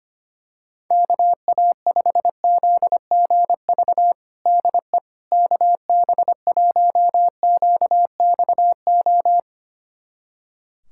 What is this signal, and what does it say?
Morse code, heard as a keyed tone, about 25 wpm, 700 Hz. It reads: KA5ZGV DE KB1QXO
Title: USA Callsign DE Callsign - 6 Characters - 25wpm